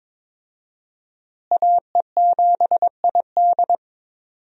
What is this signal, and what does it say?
Morse code, heard as a keyed tone, about 21 wpm, 700 Hz. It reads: AE7ID
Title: USA Call Signs - 5 Characters - Repeated 3x - 22wpm